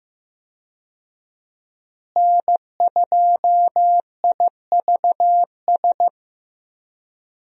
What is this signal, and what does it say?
Morse code, heard as a keyed tone, about 15 wpm, 700 Hz. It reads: N2IVS
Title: USA Call Signs - 5 Characters - Repeated 2x - 15wpm